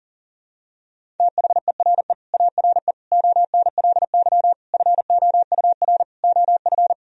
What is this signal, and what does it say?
Morse code, heard as a keyed tone, about 40 wpm, 700 Hz. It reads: THERE ARE ONLY FOUR OF
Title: Sentences from Top 1000 Words - Review Part 1 - 40wpm